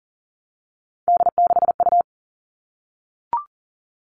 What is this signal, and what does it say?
Morse code, heard as a keyed tone, about 40 wpm, 700 Hz. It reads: D6U
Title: All Cut Numbers - 3-Digit 40wpm